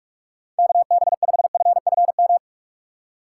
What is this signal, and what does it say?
Morse code, heard as a keyed tone, about 46 wpm, 700 Hz. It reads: KB5FFM
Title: USA Call Signs - 4x Character Spacing - 45wpm